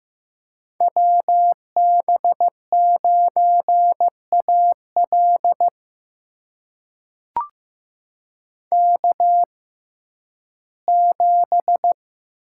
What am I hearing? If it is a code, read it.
Morse code, heard as a keyed tone, about 15 wpm, spaced as for 11 wpm, 700 Hz. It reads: WB9AL K7
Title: USA Call Signs - 5 Characters - 6x Character Spacing - 15wpm